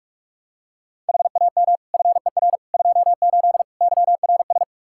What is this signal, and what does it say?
Morse code, heard as a keyed tone, about 46 wpm, 700 Hz. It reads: HAM FER 28 YRS
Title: Practice QSO - Exchange 3 - More Info - 45wpm